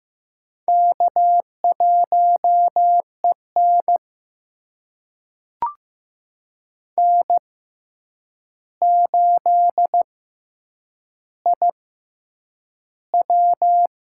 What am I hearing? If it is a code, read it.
Morse code, heard as a keyed tone, about 15 wpm, spaced as for 8 wpm, 700 Hz. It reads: K1EN N8IW
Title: USA Call Signs - 4 Characters - 6x Character Spacing - 15wpm